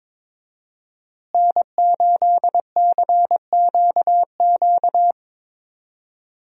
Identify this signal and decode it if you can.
Morse code, heard as a keyed tone, about 22 wpm, 700 Hz. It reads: N8CQQ
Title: USA Call Signs - 5 Characters - Repeated 3x - 22wpm